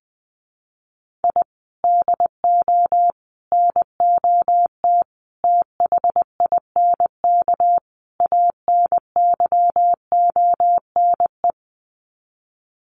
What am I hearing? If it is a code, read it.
Morse code, heard as a keyed tone, about 20 wpm, 700 Hz. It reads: I DO NOT THINK ANYONE
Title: Sets of 5 Words Encore 20wpm